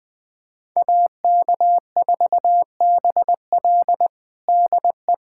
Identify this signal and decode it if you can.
Morse code, heard as a keyed tone, about 20 wpm, 700 Hz. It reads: AK4BL DE
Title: USA Callsign DE Callsign 20wpm